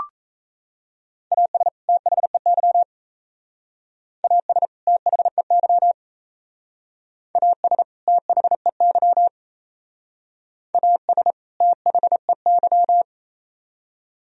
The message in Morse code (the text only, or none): AS THEY AS THEY AS THEY AS THEY